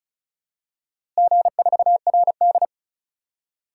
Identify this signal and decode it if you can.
Morse code, heard as a keyed tone, about 36 wpm, 700 Hz. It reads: G4RD
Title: DXCC-100 Call Signs - 4 Characters - Repeated 3x - 35wpm